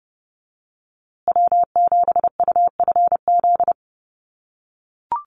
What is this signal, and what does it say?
Morse code, heard as a keyed tone, about 30 wpm, 700 Hz. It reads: W7UFZ
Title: USA Call Signs Repeated 3x 30wpm